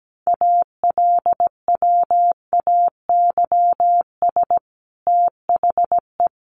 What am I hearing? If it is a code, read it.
Morse code, heard as a keyed tone, about 17 wpm, 700 Hz. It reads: ALWAYS THE
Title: Sentences from Top 300 Words 17wpm